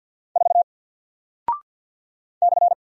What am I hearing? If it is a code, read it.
Morse code, heard as a keyed tone, about 48 wpm, 700 Hz. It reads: 4 /
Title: Single Character (Letter:Number:Punctuation:Prosign) Alt 50wpm